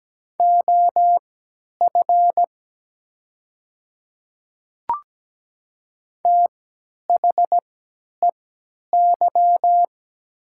Morse code, heard as a keyed tone, about 17 wpm, spaced as for 10 wpm, 700 Hz. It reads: OF THEY
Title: 3x Character Spacing - Top 200 Words - 17wpm